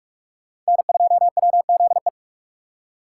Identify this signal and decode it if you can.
Morse code, heard as a keyed tone, about 43 wpm, 700 Hz. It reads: N2WZE